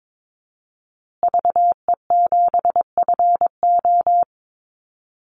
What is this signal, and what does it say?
Morse code, heard as a keyed tone, about 22 wpm, 700 Hz. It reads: VE7FO
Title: Call Signs Repeated 3x - 22wpm